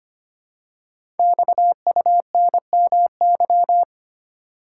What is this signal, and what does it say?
Morse code, heard as a keyed tone, about 25 wpm, 700 Hz. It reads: XUNMY
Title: Code Groups - 5 Letters - 25wpm